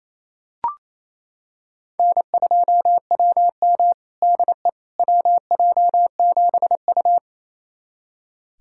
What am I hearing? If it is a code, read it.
Morse code, heard as a keyed tone, about 28 wpm, 700 Hz. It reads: N2WM DE WJ7U
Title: USA Callsign DE Callsign - 4 Characters - 28wpm